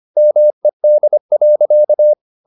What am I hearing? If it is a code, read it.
Morse code, heard as a keyed tone, about 25 wpm, 600 Hz. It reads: MED.
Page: Ferie 25 wpm